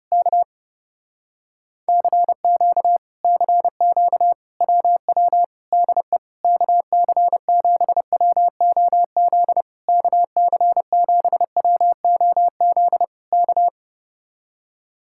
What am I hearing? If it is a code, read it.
Morse code, heard as a keyed tone, about 30 wpm, 700 Hz. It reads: K CQ CQ WW DE KC7WOZ KC7WOZ K
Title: Speed-Racing Common QSO Phrases Encore - 30wpm